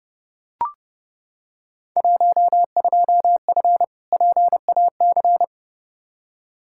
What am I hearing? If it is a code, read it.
Morse code, heard as a keyed tone, about 30 wpm, 700 Hz. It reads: 12F PAC
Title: Field Day Exchanges Encore - 30wpm